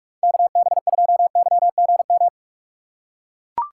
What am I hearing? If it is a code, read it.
Morse code, heard as a keyed tone, about 43 wpm, 700 Hz. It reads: KB2YGM
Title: USA Call Signs - 3x Character Spacing - 45wpm